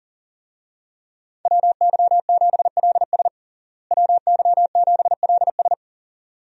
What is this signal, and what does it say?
Morse code, heard as a keyed tone, about 40 wpm, 700 Hz. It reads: WY7LS WY7LS
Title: USA Call Signs - 5 Characters - Repeated 2x - 40wpm